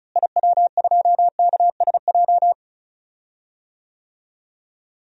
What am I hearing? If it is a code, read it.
Morse code, heard as a keyed tone, about 36 wpm, 700 Hz. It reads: IW2KSJ
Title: DXCC-100 Call Signs - 6 Characters - Repeated 2x - 35wpm